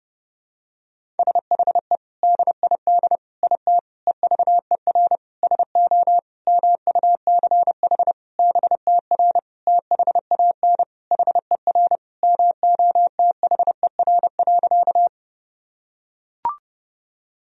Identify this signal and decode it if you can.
Morse code, heard as a keyed tone, about 30 wpm, 700 Hz. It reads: SHE DID IT EVER SO MUCH BTR THAN HER MOTHER.
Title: Sentences from Top 400 Words and Abbreviations 30wpm